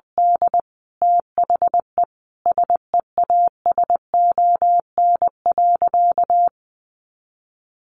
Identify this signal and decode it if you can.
Morse code, heard as a keyed tone, about 20 wpm, 700 Hz. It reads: D THE SEASON.
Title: Sentences from Top 1500 Words 20wpm